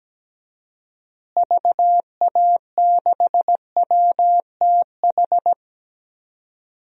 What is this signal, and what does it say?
Morse code, heard as a keyed tone, about 17 wpm, 700 Hz. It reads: VA6WTH
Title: DXCC-100 Call Signs - 6 Characters - Repeated 2x - 17wpm